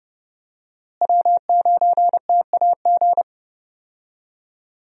Morse code, heard as a keyed tone, about 30 wpm, 700 Hz. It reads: W9TAG
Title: USA Call Signs - 5 Characters - Repeated 3x - 30wpm